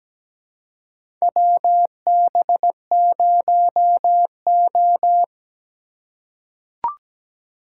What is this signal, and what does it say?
Morse code, heard as a keyed tone, about 17 wpm, 700 Hz. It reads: WB0O
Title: USA Call Signs - 4 Characters - 4x Character Spacing - 17wpm